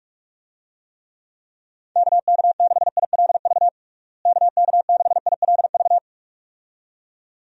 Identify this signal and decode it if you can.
Morse code, heard as a keyed tone, about 46 wpm, 700 Hz. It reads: KK6ILV KK6ILV
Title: USA Call Signs - 6 Characters - Repeated 2x - 45wpm